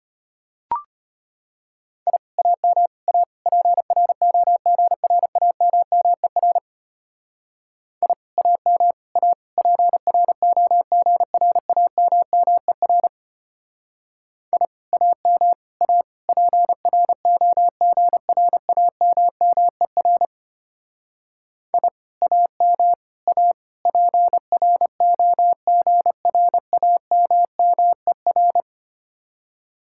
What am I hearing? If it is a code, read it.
Morse code, heard as a keyed tone, about 31 wpm, 700 Hz. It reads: I AM A PROGRAMMER I AM A PROGRAMMER I AM A PROGRAMMER I AM A PROGRAMMER
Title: Speed-Racing Common QSO Phrases - 25wpm